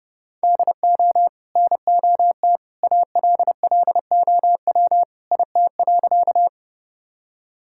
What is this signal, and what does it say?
Morse code, heard as a keyed tone, about 32 wpm, 700 Hz. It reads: DO NOT ALLOW IT.
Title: Sentences from Top 1000 Words - Review Part 5 - 30wpm